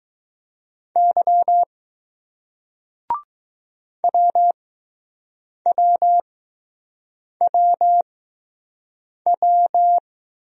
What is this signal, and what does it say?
Morse code, heard as a keyed tone, about 20 wpm, spaced as for 6 wpm, 700 Hz. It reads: Y WWWW